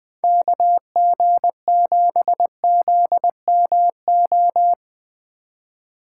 Morse code, heard as a keyed tone, about 20 wpm, 700 Hz. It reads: KG7ZMO
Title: USA Call Signs - 6 Characters - 2x Character Spacing - 20wpm